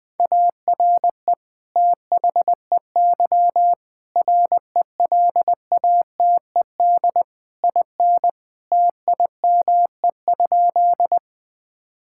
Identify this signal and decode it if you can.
Morse code, heard as a keyed tone, about 20 wpm, 700 Hz. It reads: ARE THEY RELATED IN TIME?
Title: Sentences from Top 1000 Words - Review Part 5 - 20wpm